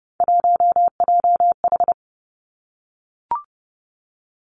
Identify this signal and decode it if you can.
Morse code, heard as a keyed tone, about 30 wpm, 700 Hz. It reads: 1JH
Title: Three Challenging Letter-Number Combinations 30wpm